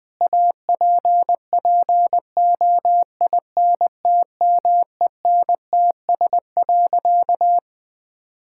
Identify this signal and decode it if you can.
Morse code, heard as a keyed tone, about 21 wpm, 700 Hz. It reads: APPOINTMENTS.
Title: Difficult Sentences with Hints - 20wpm